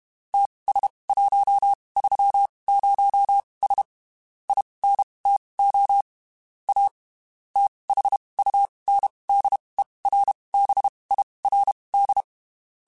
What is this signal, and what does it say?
Morse code, heard as a keyed tone, about 32 wpm, spaced as for 23 wpm, 800 Hz. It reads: TS130S INTO A THUNDERBIRD
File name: (spaced) / QSOs Spaced-mp3s / WD0BGP 1 @30 sp